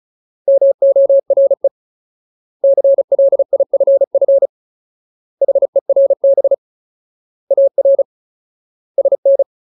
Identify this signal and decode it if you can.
Morse code, heard as a keyed tone, about 35 wpm, 550 Hz. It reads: MORE CLIFF HERB AR SN
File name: Ultimate Word List / 35WPM Ultimate